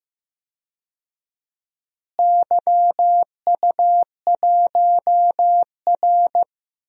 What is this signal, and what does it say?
Morse code, heard as a keyed tone, about 15 wpm, 700 Hz. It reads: YU1R
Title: DXCC-100 Call Signs - 4 Characters - Repeated 2x - 15wpm